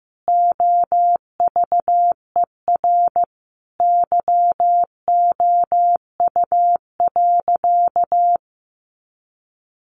Morse code, heard as a keyed tone, about 15 wpm, 700 Hz. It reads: OVER YOU.
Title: Sentences from Top 900 Words 15wpm